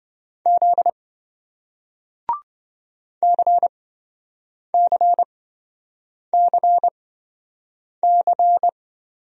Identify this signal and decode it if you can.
Morse code, heard as a keyed tone, about 27 wpm, spaced as for 7 wpm, 700 Hz. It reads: Z CCCC